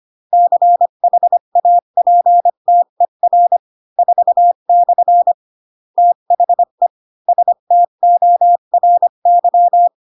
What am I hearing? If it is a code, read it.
Morse code, heard as a keyed tone, about 25 wpm, 700 Hz. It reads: CHAPTER 4/ THE STORY
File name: Pinocchio / Pinocchio ch4 20 25 ik1hge